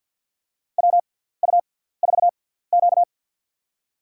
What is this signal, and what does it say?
Morse code, heard as a keyed tone, about 49 wpm, spaced as for 17 wpm, 700 Hz. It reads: WU4Q
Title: USA Call Signs - 4 Characters - 6x Character Spacing - 50wpm